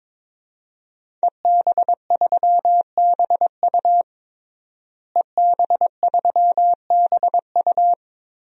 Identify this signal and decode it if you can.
Morse code, heard as a keyed tone, about 22 wpm, 700 Hz. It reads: EB3BU EB3BU
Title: DXCC-100 Call Signs - 5 Characters - Repeated 2x - 22wpm